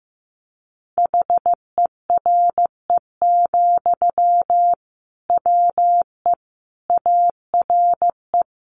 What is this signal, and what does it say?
Morse code, heard as a keyed tone, about 15 wpm, 700 Hz. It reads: HERE, WE ARE
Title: Sentences from Top 1400 Words 15wpm